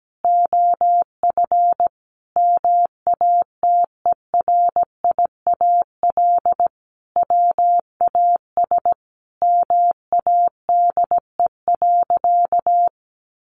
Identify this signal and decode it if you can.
Morse code, heard as a keyed tone, about 17 wpm, 700 Hz. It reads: OF MATERIAL WAS MADE.
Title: Sentences from Top 1200 Words 17wpm